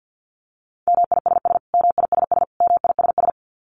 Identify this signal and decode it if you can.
Morse code, heard as a keyed tone, about 50 wpm, 700 Hz. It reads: MISS MISS MISS
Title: Top 500 Words Repeated 50wpm